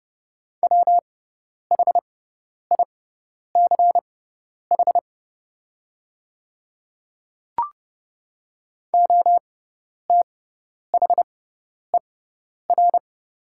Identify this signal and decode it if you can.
Morse code, heard as a keyed tone, about 30 wpm, spaced as for 10 wpm, 700 Hz. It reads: WHICH OTHER
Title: 6x Character Spacing - Top 100 Words - 30wpm